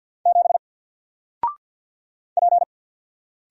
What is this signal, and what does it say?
Morse code, heard as a keyed tone, about 48 wpm, 700 Hz. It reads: 7 P
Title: Single Letter-Number Alt 50wpm